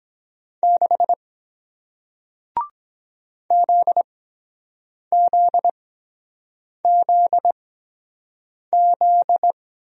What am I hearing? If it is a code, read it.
Morse code, heard as a keyed tone, about 24 wpm, 700 Hz. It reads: 6 Z Z Z Z